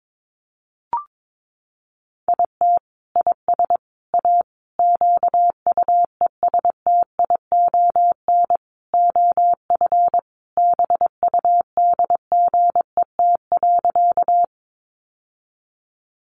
Morse code, heard as a keyed tone, about 22 wpm, 700 Hz. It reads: IT IS A QUESTION OF BUDGET.
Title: Sentences from Top 1200 Words 22wpm